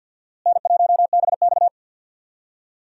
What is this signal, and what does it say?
Morse code, heard as a keyed tone, about 47 wpm, 700 Hz. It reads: N1BX
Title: USA Call Signs - 4 Characters - Repeated 3x - 50wpm